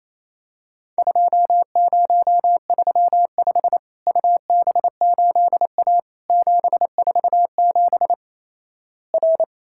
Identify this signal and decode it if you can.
Morse code, heard as a keyed tone, about 28 wpm, 700 Hz. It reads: 2035 UB8A 747 R
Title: Worked All Europe DX Contest - QTC Exchange - 28wpm